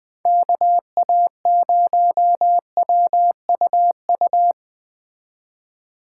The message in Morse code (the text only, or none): KA0WUU